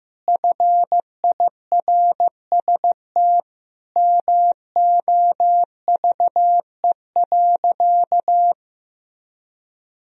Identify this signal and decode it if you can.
Morse code, heard as a keyed tone, about 15 wpm, 700 Hz. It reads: FIRST MOVE.
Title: Sentences from Top 1000 Words - Review Part 1 - 15wpm